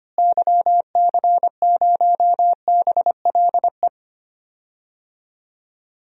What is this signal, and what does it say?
Morse code, heard as a keyed tone, about 25 wpm, 700 Hz. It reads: YC0BLE
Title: DXCC-100 Call Signs - 6 Characters - Repeated 3x - 25wpm